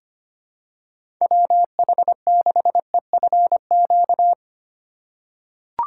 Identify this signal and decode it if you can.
Morse code, heard as a keyed tone, about 25 wpm, 700 Hz. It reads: WH6EFQ
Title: USA Call Signs - 3x Character Spacing - 25wpm